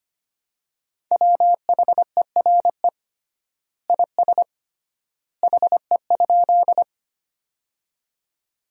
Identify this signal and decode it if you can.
Morse code, heard as a keyed tone, about 25 wpm, 700 Hz. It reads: WHERE IS HE?
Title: 3x Word-Spacing - Sentences from Top 200 Words - 25wpm